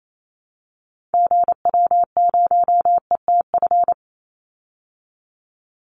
Morse code, heard as a keyed tone, about 28 wpm, 700 Hz. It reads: GW0ETF
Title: Call Signs Repeated 3x - 28wpm